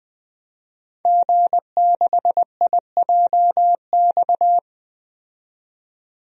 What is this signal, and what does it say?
Morse code, heard as a keyed tone, about 20 wpm, 700 Hz. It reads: G6IJX
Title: DXCC-100 Call Signs - 5 Characters - 20wpm